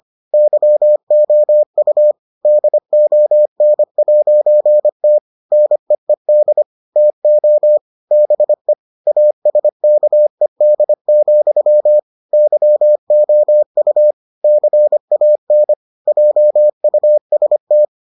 Morse code, heard as a keyed tone, about 25 wpm, 600 Hz. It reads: YOU DON'T NEED TO BE ASKED, YOU CAN JUST